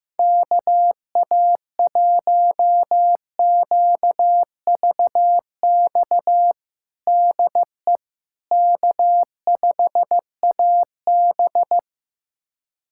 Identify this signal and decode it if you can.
Morse code, heard as a keyed tone, about 15 wpm, 700 Hz. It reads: KA1QVX DE K5AB
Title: USA Callsign DE Callsign 15wpm